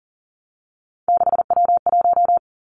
Morse code, heard as a keyed tone, about 41 wpm, 690 Hz. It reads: 6W1
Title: Three Challenging Letter-Number Combinations 40wpm